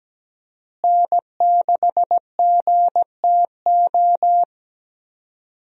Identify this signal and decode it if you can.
Morse code, heard as a keyed tone, about 17 wpm, 700 Hz. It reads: N6GTO
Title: USA Call Signs - 5 Characters - Repeated 3x - 17wpm